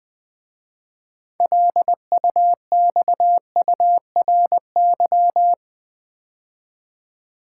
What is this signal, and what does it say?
Morse code, heard as a keyed tone, about 20 wpm, 700 Hz. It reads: LUXURY
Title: 6 Letter Words - First 3 Letters Given - 20wpm